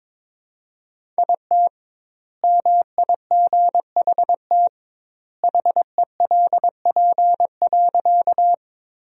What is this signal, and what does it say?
Morse code, heard as a keyed tone, about 22 wpm, 700 Hz. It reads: IT MIGHT HELP.
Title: 2x Word-Spacing - Sentences from Top 300 Words - 22wpm